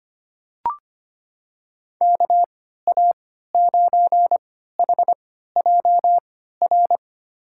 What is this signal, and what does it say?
Morse code, heard as a keyed tone, about 25 wpm, spaced as for 14 wpm, 700 Hz. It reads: KA9HJR
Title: USA Call Signs - 6 Characters - 3x Character Spacing - 25wpm